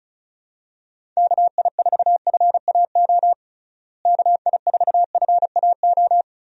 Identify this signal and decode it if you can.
Morse code, heard as a keyed tone, about 34 wpm, 700 Hz. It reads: KI4FAO KI4FAO
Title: USA Call Signs - 6 Characters - Repeated 3x - 35wpm